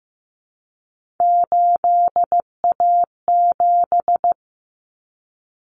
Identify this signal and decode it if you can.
Morse code, heard as a keyed tone, about 15 wpm, 700 Hz. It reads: 8A7
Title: Common Cut Numbers - 3-Digit 15wpm